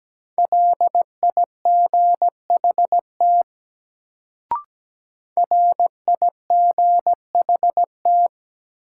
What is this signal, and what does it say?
Morse code, heard as a keyed tone, about 17 wpm, 700 Hz. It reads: LIGHT RIGHT
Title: Sentences from Top 200 Words - New Word Drill - 17wpm